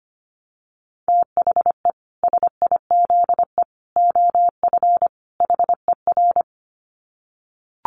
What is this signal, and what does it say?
Morse code, heard as a keyed tone, about 25 wpm, 700 Hz. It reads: THE SIZE OF HER
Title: Dit Challenge 25wpm